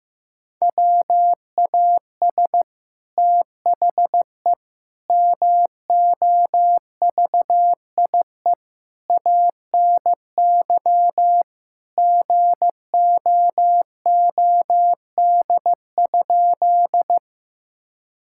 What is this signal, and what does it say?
Morse code, heard as a keyed tone, about 16 wpm, 700 Hz. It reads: WAS THE MOVIE ANY GOOD?
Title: Sentences from Top 1000 Words - Review Part 5 - 15wpm